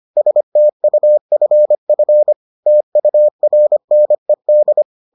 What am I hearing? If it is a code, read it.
Morse code, heard as a keyed tone, about 25 wpm, 600 Hz. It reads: STUFF TURNED